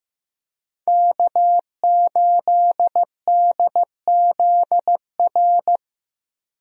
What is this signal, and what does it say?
Morse code, heard as a keyed tone, about 15 wpm, 700 Hz. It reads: K8DZR
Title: USA Call Signs - 5 Characters - Repeated 2x - 15wpm